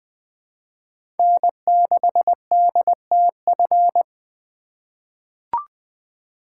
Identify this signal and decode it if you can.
Morse code, heard as a keyed tone, about 20 wpm, 700 Hz. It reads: N6DTF
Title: USA Call Signs - 5 Characters - 2x Character Spacing - 20wpm